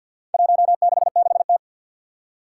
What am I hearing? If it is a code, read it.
Morse code, heard as a keyed tone, about 50 wpm, 700 Hz. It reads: 166T
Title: Four Challenging Letter-Number Combinations 50wpm